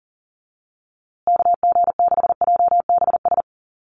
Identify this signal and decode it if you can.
Morse code, heard as a keyed tone, about 40 wpm, 700 Hz. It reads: KG6JBS
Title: USA Call Signs Repeated 3x 40wpm